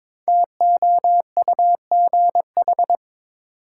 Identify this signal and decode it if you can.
Morse code, heard as a keyed tone, about 22 wpm, 700 Hz. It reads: TOUGH
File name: 5 Letter Words - First 3 Letters Given - 22wpm